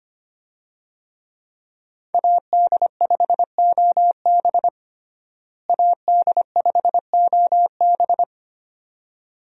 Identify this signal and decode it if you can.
Morse code, heard as a keyed tone, about 25 wpm, 700 Hz. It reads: AD5OB AD5OB
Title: USA Call Signs - 5 Characters - Repeated 2x - 25wpm